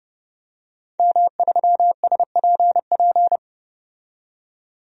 Morse code, heard as a keyed tone, about 30 wpm, 700 Hz. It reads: M3SPP
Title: DXCC-100 Call Signs - 5 Characters - 30wpm